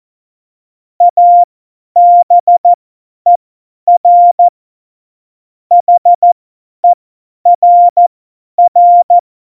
Morse code, heard as a keyed tone, about 14 wpm, spaced as for 10 wpm, 700 Hz. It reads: ABER HERR
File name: Morsekurs / Witze / Witz 04. Februar 2020, 50 F70